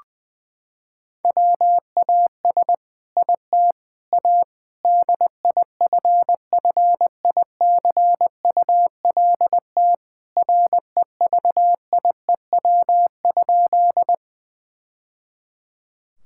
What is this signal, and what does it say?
Morse code, heard as a keyed tone, about 20 wpm, 700 Hz. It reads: WAS IT A DIFFICULT REVIEW?
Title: Sentences from Top 1000 Words - Review Part 6 - 20wpm